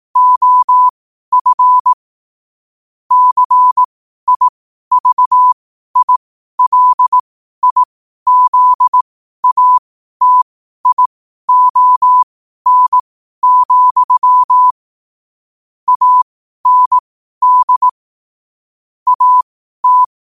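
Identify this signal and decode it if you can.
Morse code, heard as a keyed tone, about 18 wpm, spaced as for 12 wpm, 1000 Hz. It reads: OF CIVILIZATION, AND AT
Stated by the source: 12 wpm Morse code.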